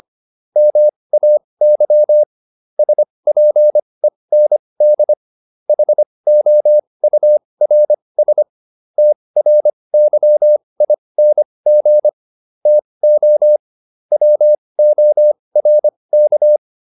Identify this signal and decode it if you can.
Morse code, heard as a keyed tone, about 25 wpm, spaced as for 20 wpm, 600 Hz. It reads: MAY SPEND HOURS TRYING TO WORK